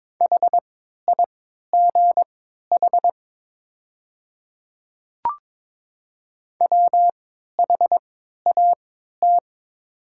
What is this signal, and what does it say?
Morse code, heard as a keyed tone, about 22 wpm, spaced as for 13 wpm, 690 Hz. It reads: HIGH WHAT
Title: 3x Character Spacing - Top 200 Words - 22wpm